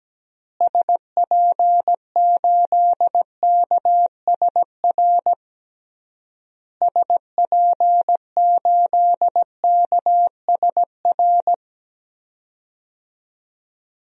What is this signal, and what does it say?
Morse code, heard as a keyed tone, about 17 wpm, 700 Hz. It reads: SP8KSR SP8KSR
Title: DXCC-100 Call Signs - 6 Characters - Repeated 3x - 17wpm